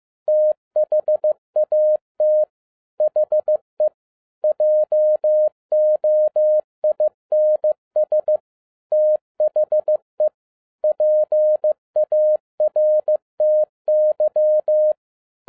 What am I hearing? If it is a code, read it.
Morse code, heard as a keyed tone, about 15 wpm, 610 Hz. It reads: THAT HE JOINS THE PARTY
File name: 2011-07-mp3 / n47 JOHN K3TN 15wpm